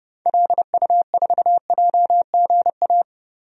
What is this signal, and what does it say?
Morse code, heard as a keyed tone, about 30 wpm, 700 Hz. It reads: LU4JGA
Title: DXCC-100 Call Signs - 6 Characters - Repeated 3x - 30wpm